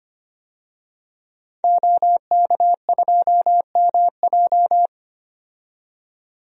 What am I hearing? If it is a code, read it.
Morse code, heard as a keyed tone, about 25 wpm, 700 Hz. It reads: OK2MJ
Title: DXCC-100 Call Signs - 5 Characters - 25wpm